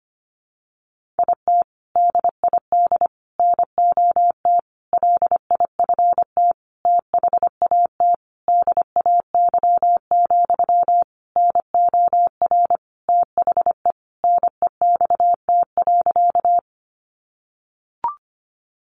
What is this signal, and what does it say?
Morse code, heard as a keyed tone, about 25 wpm, 700 Hz. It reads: IT DID NOT LIFT THAT DAY, NOR THE NEXT.
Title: Sentences from Top 1500 Words 25wpm